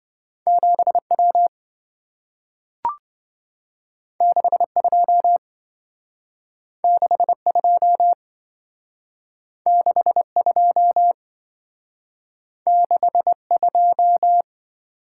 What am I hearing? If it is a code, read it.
Morse code, heard as a keyed tone, about 28 wpm, 700 Hz. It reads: 7W 62 62 62 62